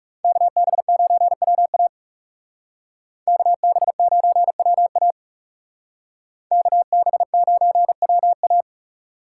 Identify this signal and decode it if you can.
Morse code, heard as a keyed tone, about 38 wpm, 700 Hz. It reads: KB9WA KB9WA KB9WA